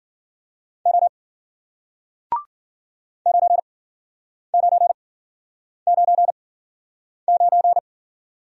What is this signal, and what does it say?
Morse code, heard as a keyed tone, about 56 wpm, spaced as for 9 wpm, 700 Hz. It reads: O 9999